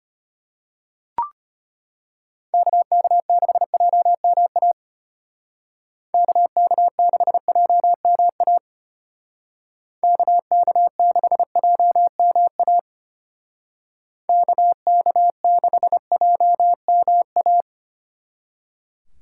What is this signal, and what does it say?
Morse code, heard as a keyed tone, about 31 wpm, 700 Hz. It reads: KK6JMA KK6JMA KK6JMA KK6JMA
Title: Speed Racing - USA 2x3 Call Signs 25wpm